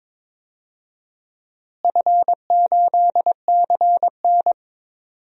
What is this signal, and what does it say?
Morse code, heard as a keyed tone, about 22 wpm, 700 Hz. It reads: F8CN
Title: DXCC-100 Call Signs - 4 Characters - Repeated 3x - 22wpm